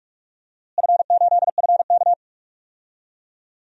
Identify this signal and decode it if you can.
Morse code, heard as a keyed tone, about 42 wpm, 700 Hz. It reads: F8FK
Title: DXCC-100 Call Signs - 4 Characters - Repeated 2x - 45wpm